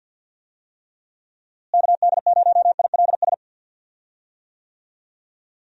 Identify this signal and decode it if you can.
Morse code, heard as a keyed tone, about 48 wpm, 700 Hz. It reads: KD0ILS
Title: USA Call Signs - 6 Characters - Repeated 3x - 50wpm